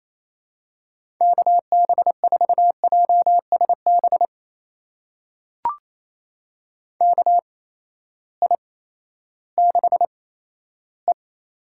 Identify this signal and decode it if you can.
Morse code, heard as a keyed tone, about 28 wpm, spaced as for 14 wpm, 700 Hz. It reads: KB4JSB KI6E
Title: USA Call Signs - 6 Characters - 8x Character Spacing - 28wpm